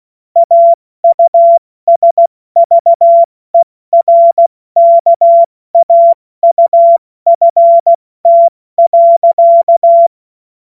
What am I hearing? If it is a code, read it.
Morse code, heard as a keyed tone, about 16 wpm, 680 Hz. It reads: AUSVERKAUFT.
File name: Morsekurs / Witze / Witz 08. August 2023 70 F80